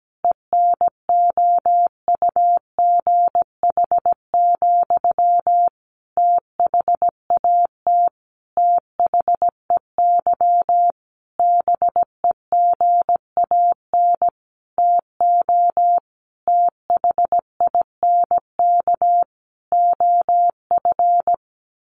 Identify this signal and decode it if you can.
Morse code, heard as a keyed tone, about 17 wpm, 700 Hz. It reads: ENOUGH, THAT THEY BEGAN TO THINK OF
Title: Sentences from Top 800 Words 17wpm